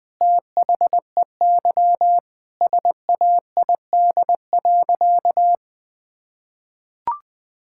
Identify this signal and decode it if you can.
Morse code, heard as a keyed tone, about 20 wpm, 700 Hz. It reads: THEY SAID.
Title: Sentences with 2-Word Phrases - 20wpm